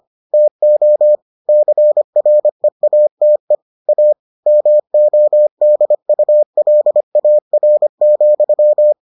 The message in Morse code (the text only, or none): TO CREATE A MODULAR,